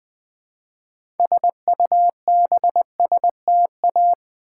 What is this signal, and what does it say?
Morse code, heard as a keyed tone, about 20 wpm, 700 Hz. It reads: SUBSTA
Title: Build A Word - 3 Letters At A Time - 9-Letter Words - 20wpm